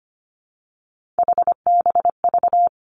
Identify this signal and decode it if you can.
Morse code, heard as a keyed tone, about 25 wpm, 700 Hz. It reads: HBV
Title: Three Challenging Letter-Number Combinations 25wpm